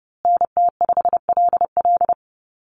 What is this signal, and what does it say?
Morse code, heard as a keyed tone, about 30 wpm, 700 Hz. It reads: NT5LL
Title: USA Call Signs Repeated 3x 30wpm